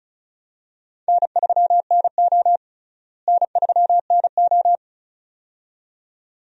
Morse code, heard as a keyed tone, about 36 wpm, 700 Hz. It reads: N3NO N3NO
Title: USA Call Signs - 4 Characters - Repeated 2x - 35wpm